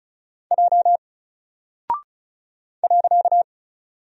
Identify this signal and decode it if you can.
Morse code, heard as a keyed tone, about 36 wpm, 700 Hz. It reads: J .
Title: Single Character (Letter:Number:Punctuation:Prosign) 35wpm